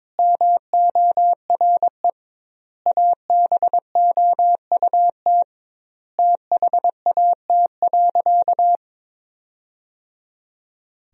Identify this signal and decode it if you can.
Morse code, heard as a keyed tone, about 22 wpm, 700 Hz. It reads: MORE ABOUT THAT.
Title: 2x Word-Spacing - Sentences from Top 200 Words - 22wpm